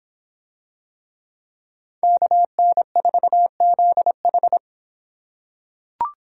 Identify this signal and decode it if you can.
Morse code, heard as a keyed tone, about 25 wpm, 700 Hz. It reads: KN4ZH